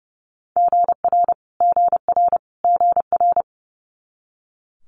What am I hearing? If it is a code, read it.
Morse code, heard as a keyed tone, about 30 wpm, 700 Hz. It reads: GR GR GR
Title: Multi-letter Phonemes Repeated 3x 30wpm